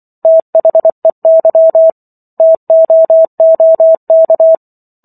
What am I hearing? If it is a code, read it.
Morse code, heard as a keyed tone, about 24 wpm, 650 Hz. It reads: THEY TOOK